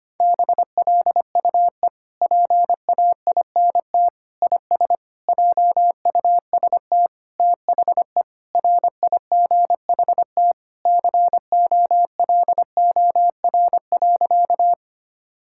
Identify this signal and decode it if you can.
Morse code, heard as a keyed tone, about 25 wpm, 700 Hz. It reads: BLUE PAINT IS JUST THE RIGHT COLOR.
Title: Sentences from Top 1000 Words - Review Part 6 - 25wpm